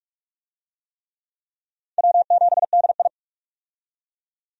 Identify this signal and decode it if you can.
Morse code, heard as a keyed tone, about 44 wpm, 700 Hz. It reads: W7DI
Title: USA Call Signs - 4 Characters - 45wpm